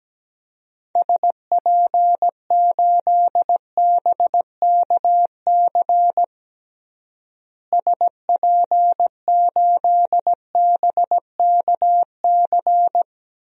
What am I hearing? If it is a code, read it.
Morse code, heard as a keyed tone, about 17 wpm, 700 Hz. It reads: SP8BKC SP8BKC
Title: DXCC-100 Call Signs - 6 Characters - Repeated 2x - 17wpm